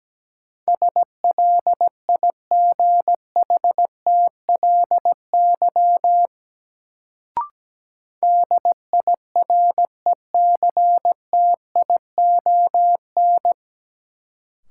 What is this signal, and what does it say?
Morse code, heard as a keyed tone, about 17 wpm, 700 Hz. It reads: SLIGHTLY DIRECTION
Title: Sentences from Top 1300 Words - New Word Drill Alt - 17wpm